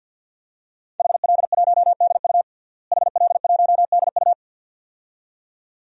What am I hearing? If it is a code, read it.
Morse code, heard as a keyed tone, about 49 wpm, 700 Hz. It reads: HL1DU HL1DU
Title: DXCC-100 Call Signs - 5 Characters - Repeated 2x - 50wpm